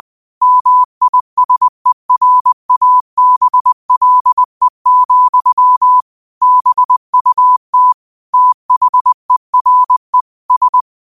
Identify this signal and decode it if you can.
Morse code, heard as a keyed tone, about 20 wpm, 1000 Hz. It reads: MISERABLE, BUT THERE S